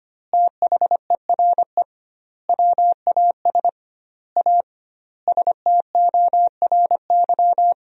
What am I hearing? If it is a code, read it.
Morse code, heard as a keyed tone, about 25 wpm, 700 Hz. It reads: THERE WAS A STORY
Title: 2x Word-Spacing - Sentences from Top 500 Words - 25wpm